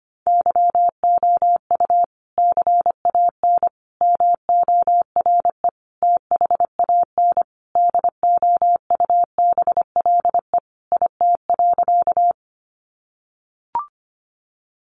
Morse code, heard as a keyed tone, about 25 wpm, 700 Hz. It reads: YOU CAN MORE THAN DOUBLE IT.
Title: Sentences from Top 1400 Words 25wpm